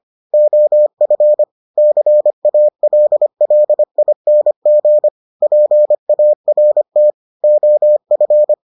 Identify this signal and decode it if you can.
Morse code, heard as a keyed tone, about 25 wpm, 600 Hz. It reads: OF CALLING PART OF